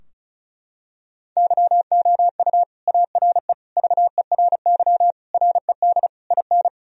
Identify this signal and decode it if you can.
Morse code, heard as a keyed tone, about 36 wpm, 700 Hz. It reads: YOU ARE VERY RED IN
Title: Sentences from Top 1000 Words - Review Part 2 - 35wpm